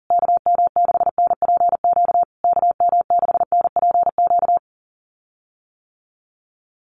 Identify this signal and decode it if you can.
Morse code, heard as a keyed tone, about 40 wpm, 700 Hz. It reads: KM6NPQ KM6NPQ
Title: USA Call Signs Repeated 3x 40wpm